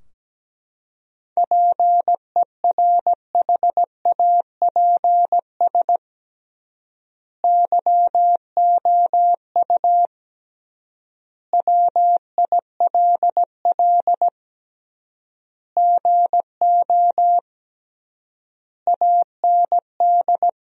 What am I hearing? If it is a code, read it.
Morse code, heard as a keyed tone, about 17 wpm, 700 Hz. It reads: PERHAPS YOU WILL GO AND
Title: 3x Word-Spacing - Sentences from Top 500 Words - 17wpm